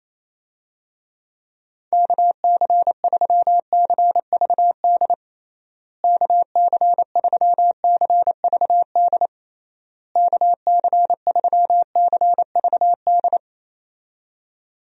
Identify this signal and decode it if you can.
Morse code, heard as a keyed tone, about 27 wpm, 700 Hz. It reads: KC3CVD KC3CVD KC3CVD
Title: USA Call Signs - 6 Characters - Repeated 3x - 28wpm